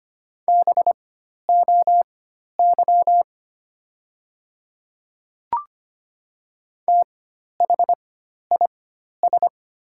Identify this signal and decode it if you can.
Morse code, heard as a keyed tone, about 25 wpm, spaced as for 12 wpm, 700 Hz. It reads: BOY THIS
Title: 4x Character Spacing - Top 200 Words - 25wpm